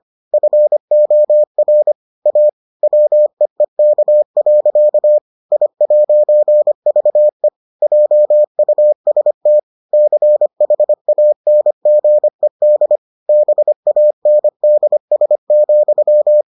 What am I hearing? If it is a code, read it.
Morse code, heard as a keyed tone, about 25 wpm, 600 Hz. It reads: FOR A WEEK. I'VE JUST CHANGED BANDS,